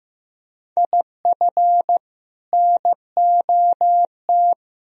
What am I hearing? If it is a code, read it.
Morse code, heard as a keyed tone, about 15 wpm, 700 Hz. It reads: IF NOT
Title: Sentences from Top 1000 Words - Review Part 3 - 15wpm